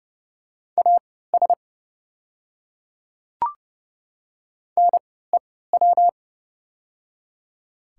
Morse code, heard as a keyed tone, about 29 wpm, spaced as for 17 wpm, 690 Hz. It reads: AS NEW
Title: 3x Character Spacing - Top 300 Words - 30wpm